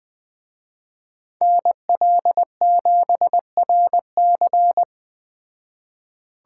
Morse code, heard as a keyed tone, about 20 wpm, 700 Hz. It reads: NL7RC